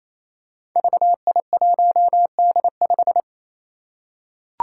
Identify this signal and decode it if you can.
Morse code, heard as a keyed tone, about 28 wpm, 700 Hz. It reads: VI1D5
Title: Five Challenging Letter-Number Combinations 28wpm